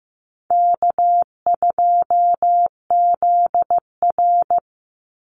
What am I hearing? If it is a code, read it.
Morse code, heard as a keyed tone, about 15 wpm, 700 Hz. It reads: K2ZR
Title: Easy Call Signs Repeated 2x - 15wpm